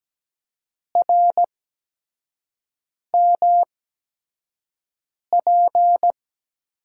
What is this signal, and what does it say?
Morse code, heard as a keyed tone, about 17 wpm, spaced as for 5 wpm, 700 Hz. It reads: RMP
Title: USA Call Signs - 5 Characters - 8x Character Spacing - 17wpm